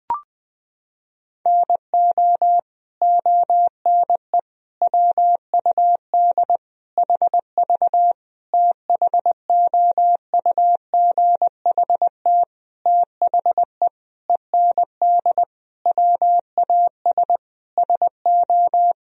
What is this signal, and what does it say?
Morse code, heard as a keyed tone, about 20 wpm, 700 Hz. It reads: NO ONE WUD HV THOUGHT THE END WAS SO
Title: Sentences from Top 300 Words and Abbreviations 20wpm